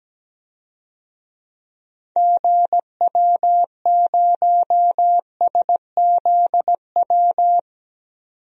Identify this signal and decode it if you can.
Morse code, heard as a keyed tone, about 17 wpm, 700 Hz. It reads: GW0SZW
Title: DXCC-100 Call Signs - 6 Characters - 17wpm